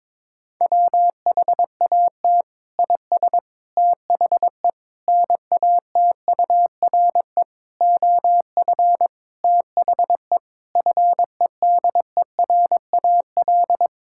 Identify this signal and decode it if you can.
Morse code, heard as a keyed tone, about 22 wpm, 700 Hz. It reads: WHAT IS THE NATURE OF THE FEDERAL
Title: Sentences from Top 1000 Words - Review Part 5 - 22wpm